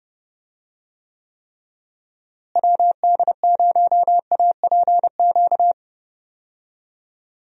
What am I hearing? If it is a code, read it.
Morse code, heard as a keyed tone, about 30 wpm, 700 Hz. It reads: WD0APQ
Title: USA Call Signs - 6 Characters - Repeated 3x - 30wpm